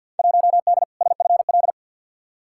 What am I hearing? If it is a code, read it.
Morse code, heard as a keyed tone, about 51 wpm, 700 Hz. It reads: 1D SFL
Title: Field Day Exchanges - 50wpm